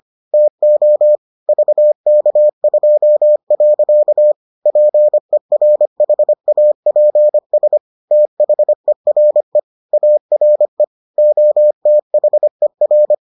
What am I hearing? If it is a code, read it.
Morse code, heard as a keyed tone, about 25 wpm, 600 Hz. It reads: TO VK2. PERHAPS THERE ARE OTHER